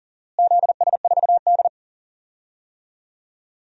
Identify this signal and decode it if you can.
Morse code, heard as a keyed tone, about 39 wpm, 700 Hz. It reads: ZS4D
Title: DXCC-100 Call Signs - 4 Characters - Repeated 3x - 40wpm